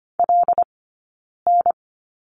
Morse code, heard as a keyed tone, about 25 wpm, 700 Hz. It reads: L N
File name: course / 25:10wpm / 092 Lesson 16d - Letter c - Two Characters 25wpm